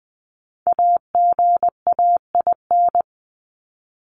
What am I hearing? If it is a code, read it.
Morse code, heard as a keyed tone, about 20 wpm, 700 Hz. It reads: AGAIN
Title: Top 200 Words 20wpm